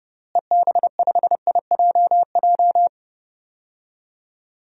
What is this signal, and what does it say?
Morse code, heard as a keyed tone, about 30 wpm, 700 Hz. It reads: EB5IJJ
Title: DXCC-100 Call Signs - 6 Characters - Repeated 2x - 30wpm